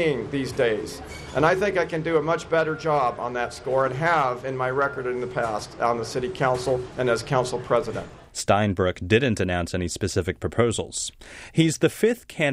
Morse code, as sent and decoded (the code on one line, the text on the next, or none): none